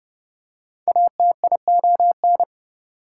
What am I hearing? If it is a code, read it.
Morse code, heard as a keyed tone, about 30 wpm, 700 Hz. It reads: ATION
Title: Prefixes and Suffixes 30wpm